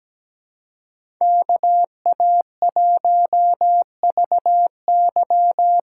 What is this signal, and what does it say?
Morse code, heard as a keyed tone, about 17 wpm, 700 Hz. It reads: KA1VY
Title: USA Call Signs - 6 Characters - 6x Character Spacing - 17wpm